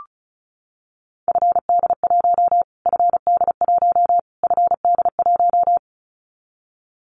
Morse code, heard as a keyed tone, about 34 wpm, 700 Hz. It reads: FD1 FD1 FD1
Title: Callsign Prefixes for Top 25 DXCC - Repeated 3x - 35wpm